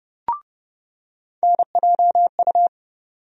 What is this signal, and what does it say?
Morse code, heard as a keyed tone, about 30 wpm, 700 Hz. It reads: NJU
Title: USA Call Sign Postfixes - 3 Letters 30wpm